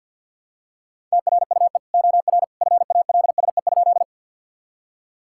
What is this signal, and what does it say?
Morse code, heard as a keyed tone, about 50 wpm, 700 Hz. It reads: TRUE OR FALSE?
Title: Sentences from Top 1600 Words 50wpm